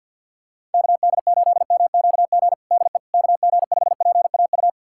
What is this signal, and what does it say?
Morse code, heard as a keyed tone, about 50 wpm, 700 Hz. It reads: KD8MQG DE KG5PAU
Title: USA Callsign DE Callsign 50wpm